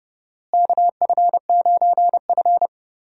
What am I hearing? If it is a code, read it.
Morse code, heard as a keyed tone, about 30 wpm, 700 Hz. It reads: KF9F
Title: USA Call Signs - 4 Characters - Repeated 2x - 30wpm